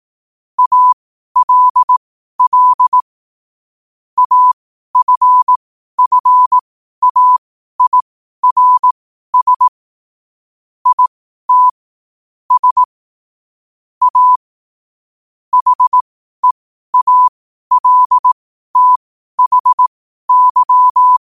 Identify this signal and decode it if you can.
Morse code, heard as a keyed tone, about 18 wpm, spaced as for 12 wpm, 1000 Hz. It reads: ALL AFFAIRS IT S A HEALTHY
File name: qotdfiles / 12wpm